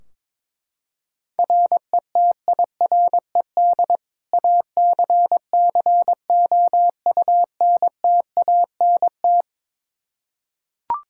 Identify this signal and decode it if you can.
Morse code, heard as a keyed tone, about 22 wpm, 700 Hz. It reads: RETIRED ACCOUNTANT
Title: K1USN SST QSO Practice 22wpm